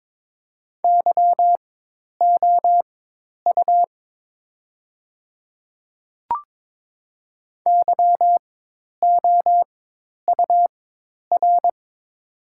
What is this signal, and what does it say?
Morse code, heard as a keyed tone, about 22 wpm, spaced as for 10 wpm, 700 Hz. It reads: YOU YOUR
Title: 4x Character Spacing - Top 500 Words - 22wpm